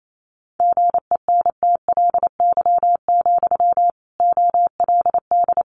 Morse code, heard as a keyed tone, about 28 wpm, 700 Hz. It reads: GENTLY, OLD
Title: Sentences from Top 1500 Words 28wpm